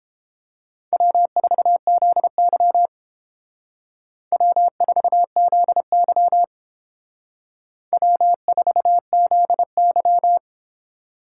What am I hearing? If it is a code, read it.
Morse code, heard as a keyed tone, about 30 wpm, 700 Hz. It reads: W4ZY W4ZY W4ZY